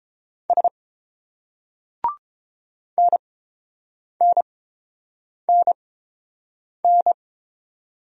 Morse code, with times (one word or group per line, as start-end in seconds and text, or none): 0.5-0.7 S
3.0-7.1 NNNN